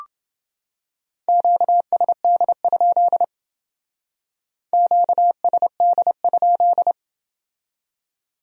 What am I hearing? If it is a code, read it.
Morse code, heard as a keyed tone, about 28 wpm, 700 Hz. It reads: QSD? QSD?